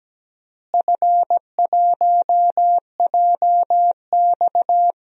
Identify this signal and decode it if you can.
Morse code, heard as a keyed tone, about 17 wpm, 700 Hz. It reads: F1JX
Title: DXCC-100 Call Signs - 4 Characters - Repeated 3x - 17wpm